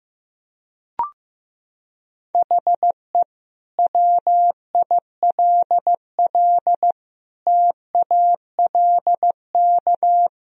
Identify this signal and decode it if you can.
Morse code, heard as a keyed tone, about 15 wpm, 700 Hz. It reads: HE WILL TALK
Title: Sentences from Top 1000 Words - Review Part 3 - 15wpm